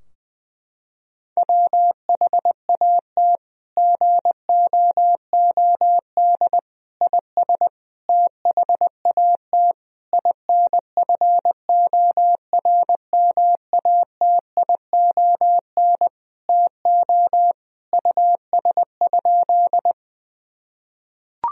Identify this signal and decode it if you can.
Morse code, heard as a keyed tone, about 20 wpm, 700 Hz. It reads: WHAT GOOD IS THAT INFORMATION TO US?
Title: Sentences from Top 1000 Words - Review Part 3 - 20wpm